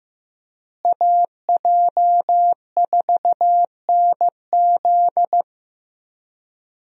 Morse code, heard as a keyed tone, about 15 wpm, 700 Hz. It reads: AJ4NZ
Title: USA Call Signs - 5 Characters - Repeated 2x - 15wpm